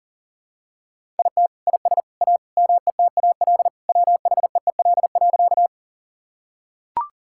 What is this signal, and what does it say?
Morse code, heard as a keyed tone, about 37 wpm, 700 Hz. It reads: IT IS A METAL WHEEL.
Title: Sentences from Top 1000 Words - Review Part 6 - 40wpm